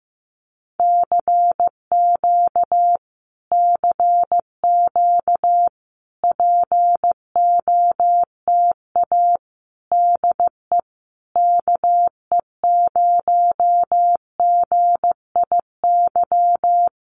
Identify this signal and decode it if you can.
Morse code, heard as a keyed tone, about 15 wpm, 700 Hz. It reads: CQ CQ POTA DE KE0GIY
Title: Common QSO Phrases Encore 15wpm